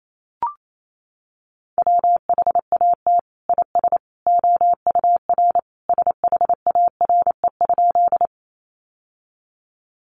Morse code, heard as a keyed tone, about 28 wpm, 700 Hz. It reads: WHAT IS OUR SHARE?
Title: Sentences from Top 900 Words 28wpm